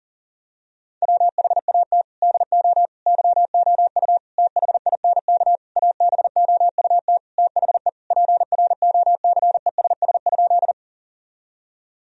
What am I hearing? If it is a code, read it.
Morse code, heard as a keyed tone, about 40 wpm, 700 Hz. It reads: WHAT DO YOU THINK ABOUT THE PROCESS?
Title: Sentences from Top 1000 Words - Review Part 3 - 40wpm